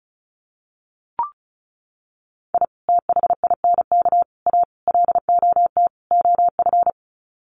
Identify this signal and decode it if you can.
Morse code, heard as a keyed tone, about 34 wpm, 700 Hz. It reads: I THINK A LOT OF
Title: Sets of 5 Words 35wpm